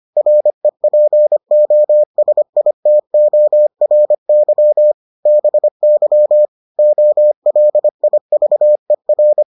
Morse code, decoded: REPOSITORY BY OLIVER

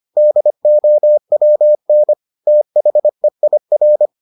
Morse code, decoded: DOWN THEIR

